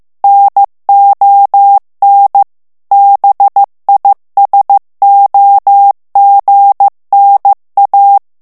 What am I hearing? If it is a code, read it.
Morse code, heard as a keyed tone, about 15 wpm, 800 Hz. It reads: NON BISOGNA
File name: morse mp3 / ITALIEN texte clair 2 vitesse 14 mots minute